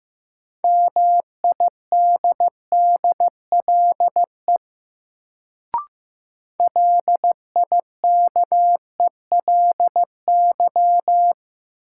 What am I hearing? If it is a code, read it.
Morse code, heard as a keyed tone, about 15 wpm, 700 Hz. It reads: MIDDLE LIKELY
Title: Sets of 2 Words - New Word Drill - 15wpm